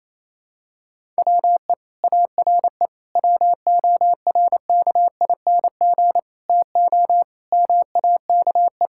Morse code, heard as a keyed tone, about 28 wpm, 700 Hz. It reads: WE ARE WORKING TO MAKE
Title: Sentences from Top 1000 Words - Review Part 6 - 28wpm